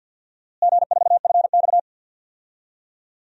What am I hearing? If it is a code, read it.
Morse code, heard as a keyed tone, about 46 wpm, 700 Hz. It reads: G4FX
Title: DXCC-100 Call Signs - 4 Characters - Repeated 3x - 50wpm